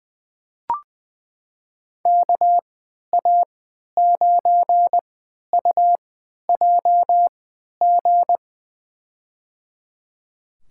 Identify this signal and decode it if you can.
Morse code, heard as a keyed tone, about 20 wpm, spaced as for 11 wpm, 700 Hz. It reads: KA9UJG
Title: USA Call Signs - 6 Characters - 3x Character Spacing - 20wpm